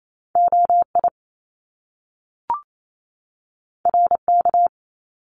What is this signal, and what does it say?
Morse code, heard as a keyed tone, about 28 wpm, 700 Hz. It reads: OI RK
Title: Top Two Letter Combinations in Words 28wpm